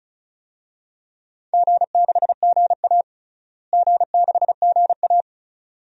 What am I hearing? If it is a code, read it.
Morse code, heard as a keyed tone, about 36 wpm, 700 Hz. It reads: G6GA G6GA
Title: DXCC-100 Call Signs - 4 Characters - Repeated 2x - 35wpm